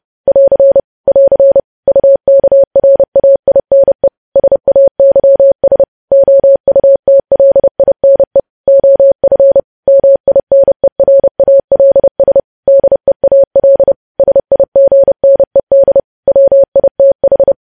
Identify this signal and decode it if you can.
Morse code, heard as a keyed tone, about 30 wpm, 550 Hz. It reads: <AR> <AR> UKRAINE SAYS OUTLINE OF MINERALS DEAL SIGNED WITH
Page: news headlines in morse code at 30 wpm